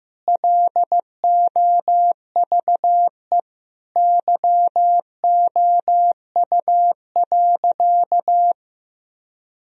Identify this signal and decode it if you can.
Morse code, heard as a keyed tone, about 15 wpm, 700 Hz. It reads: LOVE YOU.
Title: Sentences from Top 1000 Words - Review Part 1 - 15wpm